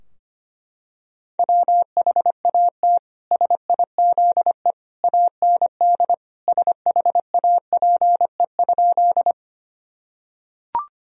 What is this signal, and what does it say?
Morse code, heard as a keyed tone, about 25 wpm, 700 Hz. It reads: WHAT SIZE AND SHAPE?
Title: Sentences from Top 1000 Words - Review Part 3 - 25wpm